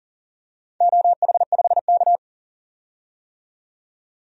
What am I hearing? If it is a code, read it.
Morse code, heard as a keyed tone, about 37 wpm, 700 Hz. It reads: OH5K
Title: DXCC-100 Call Signs - 4 Characters - 40wpm